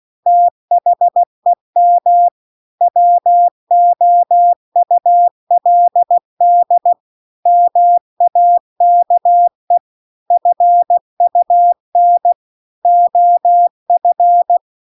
Morse code, heard as a keyed tone, about 16 wpm, 700 Hz. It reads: THEM WOULD MAKE FUN OF